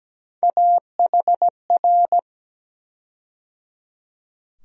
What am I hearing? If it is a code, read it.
Morse code, heard as a keyed tone, about 17 wpm, 700 Hz. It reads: AHR
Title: USA Call Sign Postfixes - 3 Letters 17wpm